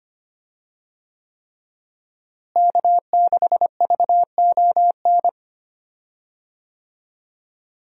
Morse code, heard as a keyed tone, about 25 wpm, 700 Hz. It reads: K6VON
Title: USA Call Signs - 5 Characters - Repeated 3x - 25wpm